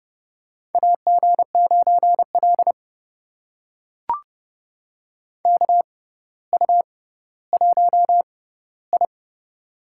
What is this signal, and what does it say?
Morse code, heard as a keyed tone, about 32 wpm, spaced as for 16 wpm, 700 Hz. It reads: AG9L KU1I
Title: USA Call Signs - 4 Characters - 6x Character Spacing - 30wpm